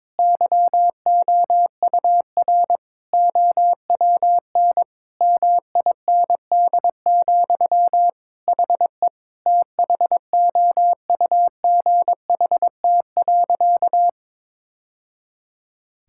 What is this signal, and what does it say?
Morse code, heard as a keyed tone, about 22 wpm, 700 Hz. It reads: YOUR OWN MIND, HE THOUGHT.
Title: Sentences from Top 1000 Words - Review Part 6 - 22wpm